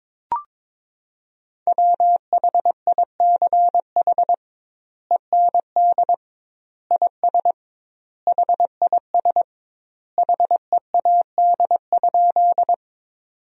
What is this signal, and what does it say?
Morse code, heard as a keyed tone, about 22 wpm, 700 Hz. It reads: WHICH END IS HIS HEAD?
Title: 2x Word-Spacing - Sentences from Top 300 Words - 22wpm